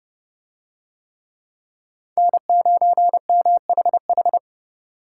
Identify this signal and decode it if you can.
Morse code, heard as a keyed tone, about 29 wpm, 700 Hz. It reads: N9MHH